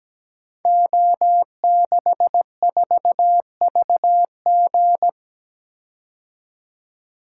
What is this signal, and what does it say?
Morse code, heard as a keyed tone, about 17 wpm, 700 Hz. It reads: O64VG
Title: Code Groups - 5 Letter:Numbers - 17wpm